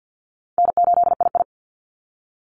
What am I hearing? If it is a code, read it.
Morse code, heard as a keyed tone, about 50 wpm, 700 Hz. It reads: N8II
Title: Easy Call Signs Repeated 2x - 50wpm